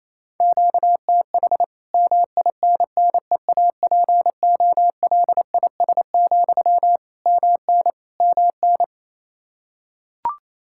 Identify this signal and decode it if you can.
Morse code, heard as a keyed tone, about 27 wpm, 700 Hz. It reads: QTH MINNEAPOLIS, MN MN
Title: Practice QSO - Exchange 1 - Beginning - 28wpm